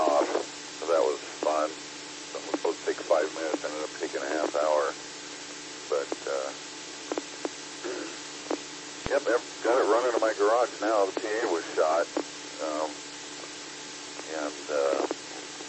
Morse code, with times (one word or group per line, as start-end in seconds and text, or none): none